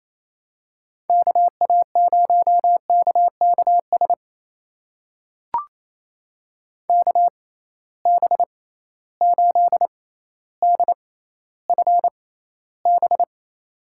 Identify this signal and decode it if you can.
Morse code, heard as a keyed tone, about 28 wpm, spaced as for 14 wpm, 700 Hz. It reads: KA0KKS KB8DFB
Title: USA Call Signs - 6 Characters - 6x Character Spacing - 28wpm